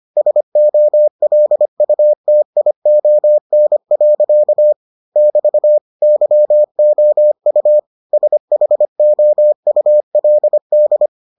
Morse code, heard as a keyed tone, about 25 wpm, 600 Hz. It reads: SOLUTION. <BT> YOU SHOULD